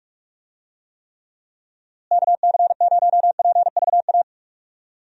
Morse code, heard as a keyed tone, about 45 wpm, 700 Hz. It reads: KC0PVA
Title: USA Call Signs - 6 Characters - Repeated 2x - 45wpm